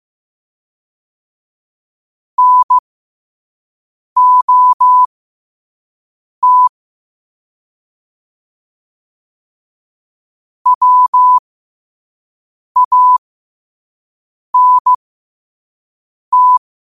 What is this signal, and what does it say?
Morse code, heard as a keyed tone, about 15 wpm, spaced as for 5 wpm, 1000 Hz. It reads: NOT WANT
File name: qotdfiles / 5wpm